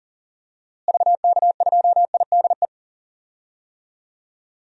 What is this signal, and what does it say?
Morse code, heard as a keyed tone, about 38 wpm, 700 Hz. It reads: VK2IDE
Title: DXCC-100 Call Signs - 6 Characters - Repeated 2x - 40wpm